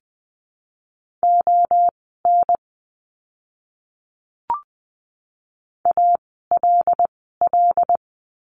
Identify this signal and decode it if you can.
Morse code, heard as a keyed tone, about 20 wpm, spaced as for 15 wpm, 700 Hz. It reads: ON ALL
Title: 101 Lesson 17f - Letter u - Words 20wpm